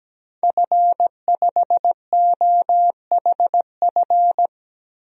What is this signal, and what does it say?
Morse code, heard as a keyed tone, about 17 wpm, 700 Hz. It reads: F5OHF
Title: DXCC-100 Call Signs - 5 Characters - 17wpm